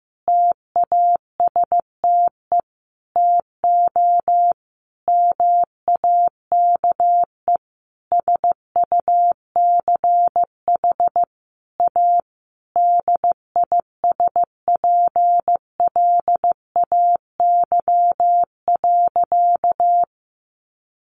Morse code, heard as a keyed tone, about 15 wpm, 700 Hz. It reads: TASTE TO MAKE SUCH A DISPLAY.
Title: Sentences from Top 1200 Words 15wpm